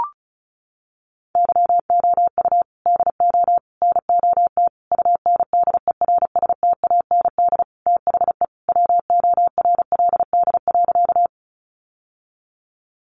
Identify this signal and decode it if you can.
Morse code, heard as a keyed tone, about 34 wpm, 700 Hz. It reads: YOU DO NOT UNDERSTAND THE WORLD.
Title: Sentences from Top 600 Words 35wpm